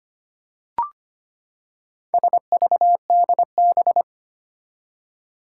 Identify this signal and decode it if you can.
Morse code, heard as a keyed tone, about 25 wpm, 700 Hz. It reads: SVDB
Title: Four Challenging Letter-Number Combinations 25wpm